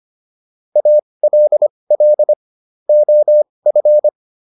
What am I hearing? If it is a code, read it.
Morse code, heard as a keyed tone, about 25 wpm, spaced as for 20 wpm, 600 Hz. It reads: ALL OF